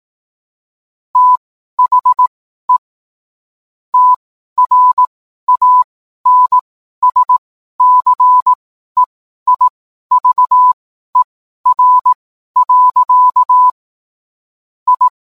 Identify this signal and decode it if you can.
Morse code, heard as a keyed tone, about 18 wpm, spaced as for 12 wpm, 1000 Hz. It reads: THE TRANSCEIVER. I